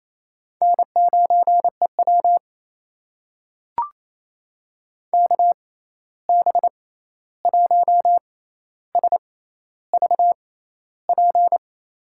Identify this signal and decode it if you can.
Morse code, heard as a keyed tone, about 28 wpm, spaced as for 13 wpm, 700 Hz. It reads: N9EW KB1SVP
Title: USA Call Signs - 6x Character Spacing - 28wpm